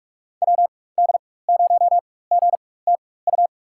Morse code, heard as a keyed tone, about 46 wpm, spaced as for 21 wpm, 700 Hz. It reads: WD0GTU
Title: USA Call Signs - 6 Characters - 4x Character Spacing - 45wpm